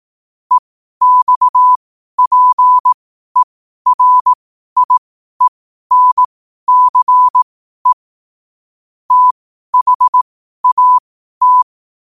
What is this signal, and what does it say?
Morse code, heard as a keyed tone, about 18 wpm, spaced as for 12 wpm, 1000 Hz. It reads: EXPERIENCE THAT
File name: qotdfiles / 12wpm